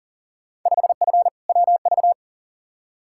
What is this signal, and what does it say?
Morse code, heard as a keyed tone, about 40 wpm, 700 Hz. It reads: 5F WV
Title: Field Day Exchanges Encore - 40wpm